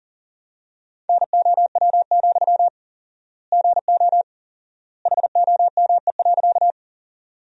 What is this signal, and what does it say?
Morse code, heard as a keyed tone, about 40 wpm, 700 Hz. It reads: NOW, GO HOME.
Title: 4x Word-Spacing - Sentences from Top 200 Words - 40wpm